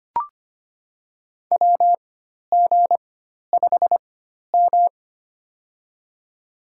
Morse code, heard as a keyed tone, about 25 wpm, spaced as for 12 wpm, 700 Hz. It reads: WG5M
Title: USA Call Signs - 4 Characters - 4x Character Spacing - 25wpm